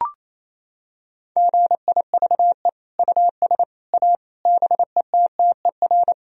Morse code, heard as a keyed tone, about 28 wpm, 700 Hz. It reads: GIVE US A BETTER
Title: Sentences from Top 1000 Words - Review Part 4 - 28wpm